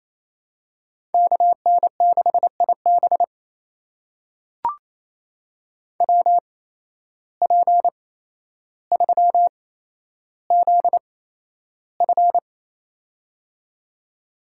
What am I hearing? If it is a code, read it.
Morse code, heard as a keyed tone, about 28 wpm, spaced as for 12 wpm, 700 Hz. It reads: KN6IB WP3ZF
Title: USA Call Signs - 5 Characters - 8x Character Spacing - 28wpm